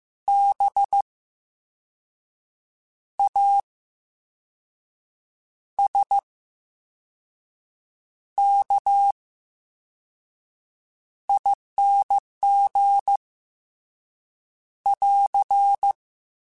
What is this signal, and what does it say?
Morse code, heard as a keyed tone, about 15 wpm, spaced as for 5 wpm, 800 Hz. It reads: BASKING <AR>
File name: Special INGs / Special INGs-mp3s / Slow ing @13